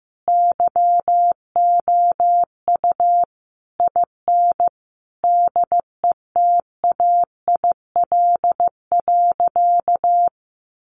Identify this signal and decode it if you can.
Morse code, heard as a keyed tone, about 15 wpm, 700 Hz. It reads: YOU IN DETAIL.
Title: Sentences from Top 1500 Words 15wpm